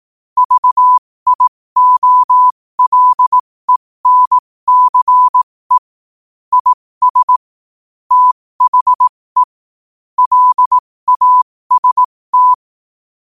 Morse code, heard as a keyed tone, about 18 wpm, spaced as for 15 wpm, 1000 Hz. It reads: VIOLENCE IS THE LAST
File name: qotdfiles / 15wpm